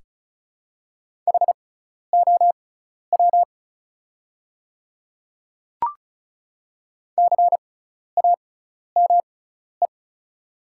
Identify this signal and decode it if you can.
Morse code, heard as a keyed tone, about 36 wpm, spaced as for 12 wpm, 690 Hz. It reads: HOW CAME
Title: 6x Character Spacing - Top 300 Words - 35wpm